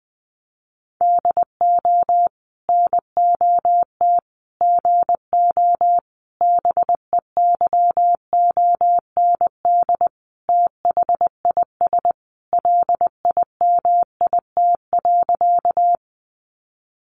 Morse code, heard as a keyed tone, about 20 wpm, 700 Hz. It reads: DO NOT GO BEYOND THIS LIMIT.
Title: Sentences from Top 1500 Words 20wpm